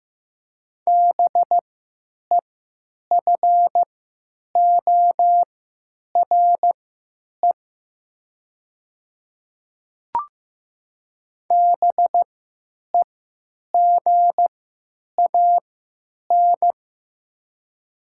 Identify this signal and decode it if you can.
Morse code, heard as a keyed tone, about 15 wpm, spaced as for 9 wpm, 700 Hz. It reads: BEFORE BEGAN
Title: 3x Character Spacing - Top 300 Words - 15wpm